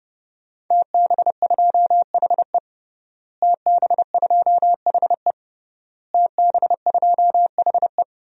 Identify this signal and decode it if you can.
Morse code, heard as a keyed tone, about 29 wpm, 700 Hz. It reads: TB2HE TB2HE TB2HE
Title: DXCC-100 Call Signs - 5 Characters - Repeated 3x - 30wpm